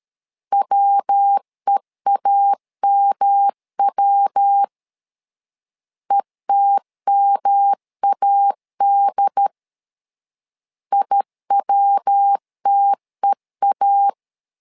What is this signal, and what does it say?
Morse code, heard as a keyed tone, about 13 wpm, 800 Hz. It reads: WEAMW ETMAD IWTEA
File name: LEZIONI / 3 MWRO / 3 MW 12wpm 1633